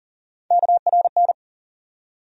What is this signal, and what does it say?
Morse code, heard as a keyed tone, about 39 wpm, 700 Hz. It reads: KRN
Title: USA Call Sign Postfixes - 3 Letters 40wpm